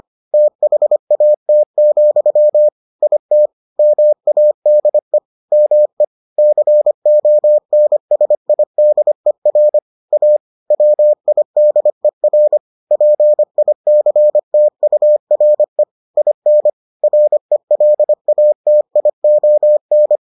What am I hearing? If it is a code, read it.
Morse code, heard as a keyed tone, about 25 wpm, 600 Hz. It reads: THAT, IT MADE ME CONSIDER A WIDER PICTURE IN RELATION